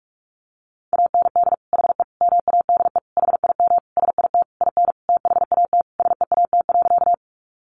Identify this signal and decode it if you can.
Morse code, heard as a keyed tone, about 44 wpm, 700 Hz. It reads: AND HE MADE HIM SIT IN THAT SEAT.
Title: Sentences from Top 1500 Words 45wpm